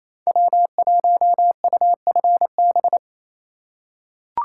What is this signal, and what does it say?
Morse code, heard as a keyed tone, about 28 wpm, 700 Hz. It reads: W1UFB
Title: USA Call Signs - 4x Character Spacing - 28wpm